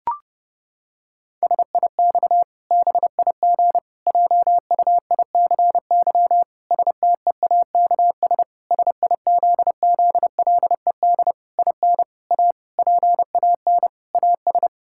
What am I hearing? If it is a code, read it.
Morse code, heard as a keyed tone, about 32 wpm, 700 Hz. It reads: SIX BIG JUICY STEAKS SIZZLED IN A PAN AS
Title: Pangram Challenge v2 - 30wpm